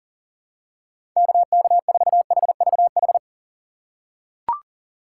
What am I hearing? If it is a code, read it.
Morse code, heard as a keyed tone, about 40 wpm, 700 Hz. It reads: KK4HVH
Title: USA Call Signs - 6 Characters - 2x Character Spacing - 40wpm